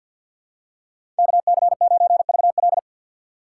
Morse code, heard as a keyed tone, about 52 wpm, 700 Hz. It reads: KC9VL
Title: USA Call Signs - 5 Characters - 4x Character Spacing - 50wpm